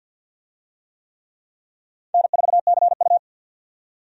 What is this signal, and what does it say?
Morse code, heard as a keyed tone, about 51 wpm, 700 Hz. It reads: N4CU
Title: USA Call Signs - 4 Characters - 50wpm